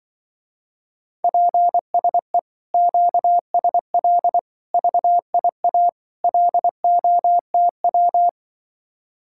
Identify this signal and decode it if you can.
Morse code, heard as a keyed tone, about 24 wpm, 700 Hz. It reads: PSE QSL VIA LOTW
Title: Speed-Racing Common QSO Phrases - 20wpm